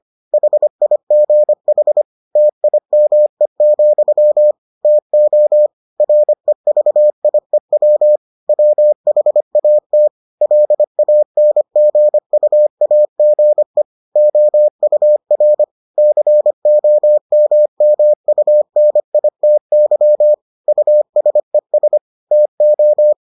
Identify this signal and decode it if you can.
Morse code, heard as a keyed tone, about 25 wpm, 600 Hz. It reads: HIGH TIME, TO REVIEW WHAT LANGUAGE OUR COMMUNITY USES TO